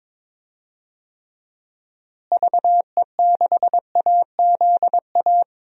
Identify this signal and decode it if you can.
Morse code, heard as a keyed tone, about 22 wpm, 700 Hz. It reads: VE6AZA
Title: DXCC-100 Call Signs - 6 Characters - Repeated 3x - 22wpm